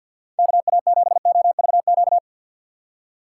Morse code, heard as a keyed tone, about 52 wpm, 700 Hz. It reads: KA7OVQ
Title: USA Call Signs - 6 Characters - 4x Character Spacing - 50wpm